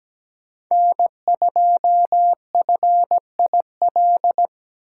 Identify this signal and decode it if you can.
Morse code, heard as a keyed tone, about 20 wpm, 700 Hz. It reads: N2FIL
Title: Speed Racing - USA 1x3 Call Signs 17wpm